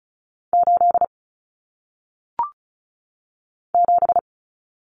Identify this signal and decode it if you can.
Morse code, heard as a keyed tone, about 36 wpm, 700 Hz. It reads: 8 7
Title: Common Cut Numbers - 1-Digit 35wpm